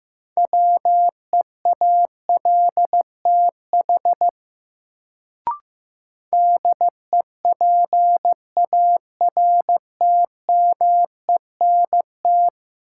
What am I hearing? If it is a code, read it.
Morse code, heard as a keyed tone, about 15 wpm, 700 Hz. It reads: WEALTH DEPARTMENT
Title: Sentences from Top 1200 Words - New Word Drill Alt - 15wpm